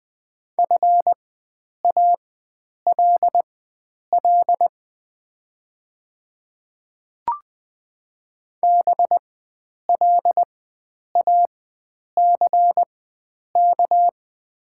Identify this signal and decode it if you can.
Morse code, heard as a keyed tone, about 20 wpm, spaced as for 9 wpm, 700 Hz. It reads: FALL BLACK
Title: 4x Character Spacing - Top 500 Words - 20wpm